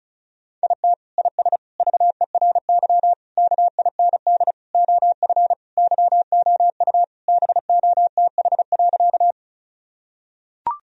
Morse code, heard as a keyed tone, about 36 wpm, 700 Hz. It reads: IT IS VERY KIND OF YOU BOTH.
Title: Sentences from Top 1000 Words - Review Part 1 - 35wpm